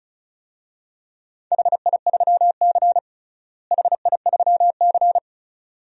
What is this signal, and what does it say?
Morse code, heard as a keyed tone, about 36 wpm, 700 Hz. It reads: HI3C HI3C
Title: DXCC-100 Call Signs - 4 Characters - Repeated 2x - 35wpm